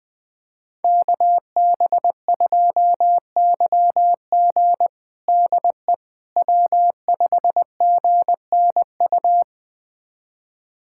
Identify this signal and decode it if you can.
Morse code, heard as a keyed tone, about 20 wpm, 700 Hz. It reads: KB2YG DE W5GNU
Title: USA Callsign DE Callsign - 5 Characters - 20wpm